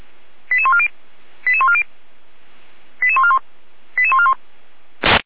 Five-tone selective call (ZVEI), heard as quick runs of five tones, tones of about 70 ms. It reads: 88140 88140 88141 88141